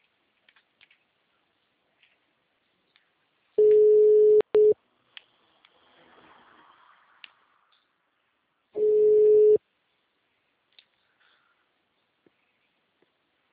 Ringback tone (Europe, 425 Hz).
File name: out- -2030- - - .31